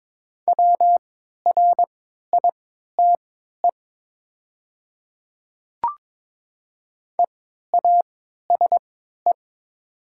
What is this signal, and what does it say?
Morse code, heard as a keyed tone, about 22 wpm, spaced as for 12 wpm, 690 Hz. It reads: WRITE EASE
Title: 3x Character Spacing - Top 500 Words - 22wpm